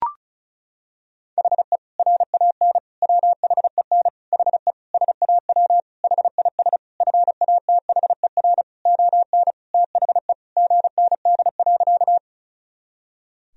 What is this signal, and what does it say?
Morse code, heard as a keyed tone, about 36 wpm, 700 Hz. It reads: HE RAN WHEN HE SAW HIS FATHER ON THE GND.
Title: Sentences from Top 500 Words and Abbreviations 35wpm